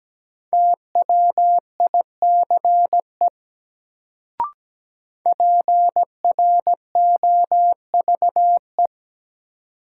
Morse code, heard as a keyed tone, about 17 wpm, 700 Hz. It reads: TWICE PROVE
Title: Sentences from Top 1600 Words - New Word Drill - 17wpm